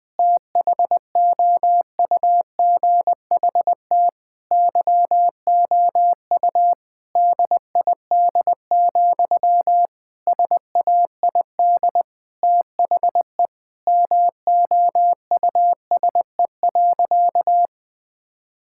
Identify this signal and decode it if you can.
Morse code, heard as a keyed tone, about 19 wpm, 700 Hz. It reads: THOUGHT YOU DID, SAID THE MOUSE.
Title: Sentences from Top 1000 Words - Review Part 5 - 20wpm